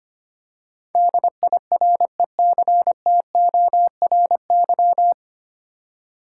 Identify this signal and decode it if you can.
Morse code, heard as a keyed tone, about 25 wpm, 700 Hz. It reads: DIRECTORY
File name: Nine Letter Words - 25wpm